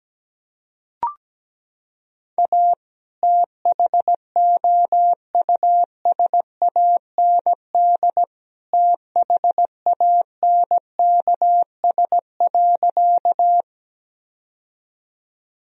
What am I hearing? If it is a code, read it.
Morse code, heard as a keyed tone, about 17 wpm, 700 Hz. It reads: A THOUSAND THANKS.
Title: Sentences from Top 1000 Words - Review Part 3 - 17wpm